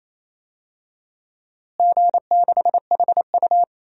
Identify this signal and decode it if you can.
Morse code, heard as a keyed tone, about 27 wpm, 700 Hz. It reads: G6HU
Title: DXCC-100 Call Signs - 4 Characters - Repeated 3x - 28wpm